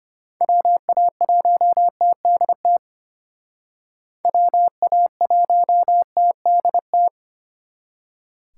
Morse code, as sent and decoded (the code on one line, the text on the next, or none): .-- .- .---- - -.. - / .-- .- .---- - -.. -
WA1TDT WA1TDT